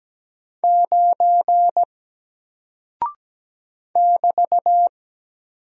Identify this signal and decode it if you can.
Morse code, heard as a keyed tone, about 17 wpm, 700 Hz. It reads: 9 <BT>
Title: Single Character (Letter:Number:Punctuation:Prosign) 17wpm